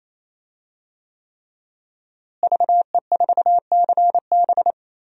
Morse code, heard as a keyed tone, about 28 wpm, 700 Hz. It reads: VE4CB
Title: DXCC-100 Call Signs - 5 Characters - Repeated 3x - 28wpm